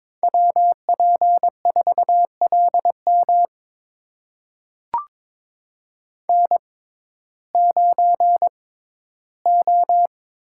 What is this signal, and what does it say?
Morse code, heard as a keyed tone, about 22 wpm, spaced as for 14 wpm, 700 Hz. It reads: WP4LM N9O
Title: USA Call Signs - 5 Characters - 6x Character Spacing - 22wpm